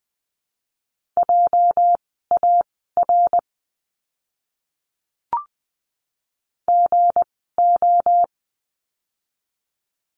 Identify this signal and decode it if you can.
Morse code, heard as a keyed tone, about 20 wpm, spaced as for 14 wpm, 700 Hz. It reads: JAR GO
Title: 206 Lesson 33f - Letter j - Words 20wpm